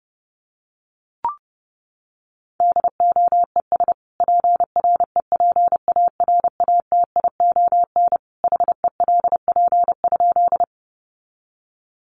Morse code, heard as a keyed tone, about 30 wpm, 700 Hz. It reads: DOES PREPARATION HELP?
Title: Sentences from Top 1400 Words 30wpm